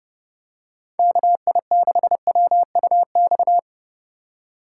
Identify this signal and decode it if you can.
Morse code, heard as a keyed tone, about 32 wpm, 700 Hz. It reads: KI6WUX
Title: USA Call Signs - 2x Character Spacing - 30wpm